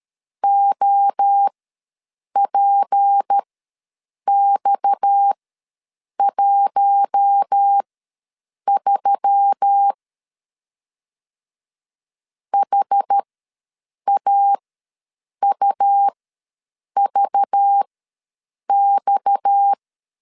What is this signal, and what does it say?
Morse code, heard as a keyed tone, about 13 wpm, spaced as for 7 wpm, 800 Hz. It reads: OPX13 HAUVX
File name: LEZIONI / 6 PKX0 / 6 PKX ws 12wpm 3106